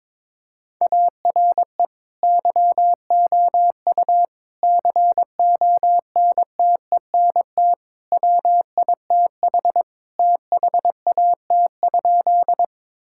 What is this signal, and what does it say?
Morse code, heard as a keyed tone, about 22 wpm, 700 Hz. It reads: ARE YOU CONTENT WITH THAT?
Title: Sentences from Top 1000 Words - Review Part 6 - 22wpm